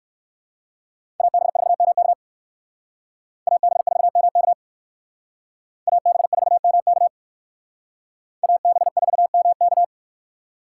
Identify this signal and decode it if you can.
Morse code, heard as a keyed tone, about 60 wpm, 700 Hz. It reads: AB4MK AB4MK AB4MK AB4MK